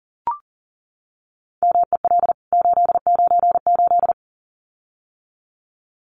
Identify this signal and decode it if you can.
Morse code, heard as a keyed tone, about 40 wpm, 700 Hz. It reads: MEL 898
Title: CWT Exchange 40wpm